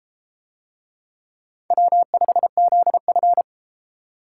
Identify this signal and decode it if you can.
Morse code, heard as a keyed tone, about 30 wpm, 700 Hz. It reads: W5ZF